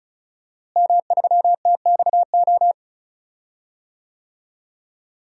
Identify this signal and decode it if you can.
Morse code, heard as a keyed tone, about 34 wpm, 700 Hz. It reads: M3TXO
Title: DXCC-100 Call Signs - 5 Characters - Repeated 2x - 35wpm